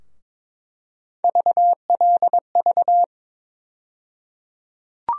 Morse code, heard as a keyed tone, about 22 wpm, 700 Hz. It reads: VLV
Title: USA Call Sign Postfixes - 3 Letters 22wpm